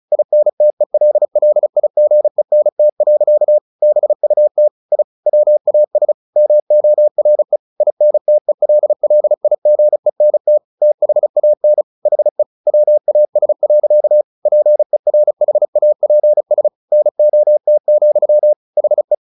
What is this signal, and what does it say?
Morse code, heard as a keyed tone, about 36 wpm, 600 Hz. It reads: INTELLIGENT. BUT I WAS MORE INTELLIGENT THAN HE WAS. PERHAPS NOT, HE
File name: the old man and the sea 2 35wpm